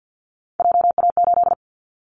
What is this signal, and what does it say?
Morse code, heard as a keyed tone, about 50 wpm, 700 Hz. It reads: JA8
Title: Callsign Prefixes for Top 25 DXCC - 50wpm